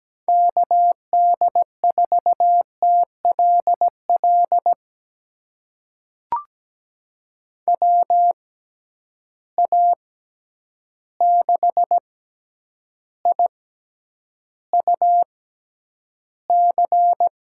Morse code, heard as a keyed tone, about 17 wpm, spaced as for 9 wpm, 700 Hz. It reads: KD4TLL WA6IUC
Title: USA Call Signs - 6 Characters - 6x Character Spacing - 17wpm